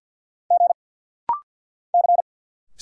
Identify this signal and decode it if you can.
Morse code, heard as a keyed tone, about 50 wpm, 690 Hz. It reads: G C
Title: Single Letters - Rapid-Fire 50wpm